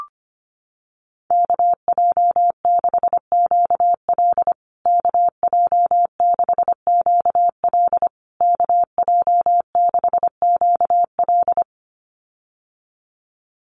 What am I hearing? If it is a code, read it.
Morse code, heard as a keyed tone, about 25 wpm, 700 Hz. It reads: KJ6QL KJ6QL KJ6QL
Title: USA Call Signs Repeated 3x 25wpm